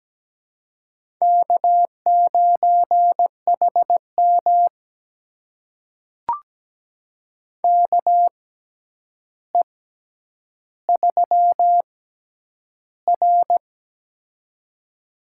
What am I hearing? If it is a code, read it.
Morse code, heard as a keyed tone, about 17 wpm, spaced as for 9 wpm, 700 Hz. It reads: K9HM KE3R
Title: USA Call Signs - 4 Characters - 6x Character Spacing - 17wpm